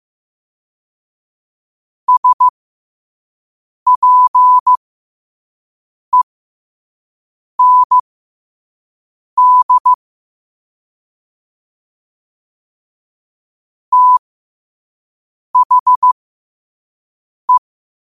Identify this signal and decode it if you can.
Morse code, heard as a keyed tone, about 15 wpm, spaced as for 5 wpm, 1000 Hz. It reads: SPEND THE
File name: qotdfiles / 5wpm